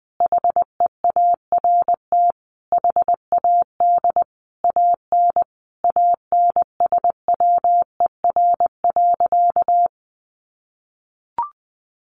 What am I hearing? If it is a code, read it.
Morse code, heard as a keyed tone, about 20 wpm, 700 Hz. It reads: HEART HAD AN ANSWER.
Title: Sentences from Top 1200 Words 20wpm